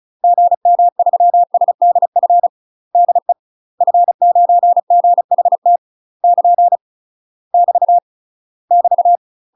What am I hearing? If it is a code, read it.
Morse code, heard as a keyed tone, about 35 wpm, 700 Hz. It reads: GM3SDF DE F9GHT <KN> <BT> <BT>
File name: UK MASTER QSOs TRAINING / 35WPM / UK MASTER QSOs 0018